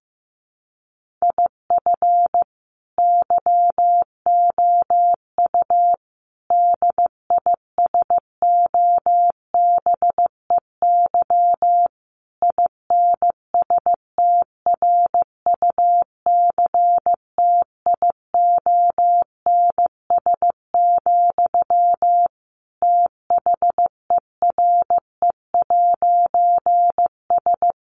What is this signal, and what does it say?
Morse code, heard as a keyed tone, about 15 wpm, 700 Hz. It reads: IF YOU DISOBEY INSTRUCTIONS, THERE'S
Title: A Sound Of Thunder 15wpm